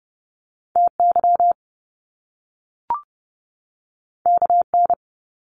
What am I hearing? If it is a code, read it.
Morse code, heard as a keyed tone, about 30 wpm, 700 Hz. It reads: TY KN
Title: Top Two Letter Combinations in Words 30wpm